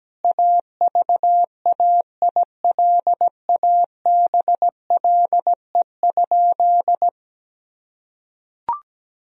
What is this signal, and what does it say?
Morse code, heard as a keyed tone, about 17 wpm, 700 Hz. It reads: AVAILABLE?
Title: Sentences from Top 1000 Words - Review Part 3 - 17wpm